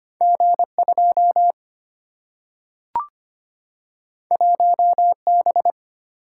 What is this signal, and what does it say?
Morse code, heard as a keyed tone, about 25 wpm, 700 Hz. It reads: G2 1B
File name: Two Letter-Number Combinations - 25wpm